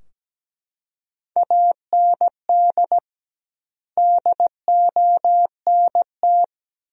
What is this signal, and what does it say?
Morse code, heard as a keyed tone, about 17 wpm, 700 Hz. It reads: AND DONT
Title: 2x Word-Spacing - Sentences from Top 400 Words - 17wpm